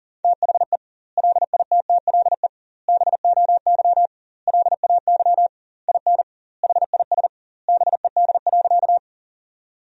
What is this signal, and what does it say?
Morse code, heard as a keyed tone, about 40 wpm, 700 Hz. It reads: THE LITTLE BOY LAY IN HIS BED.
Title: 2x Word-Spacing - Sentences from Top 500 Words - 40wpm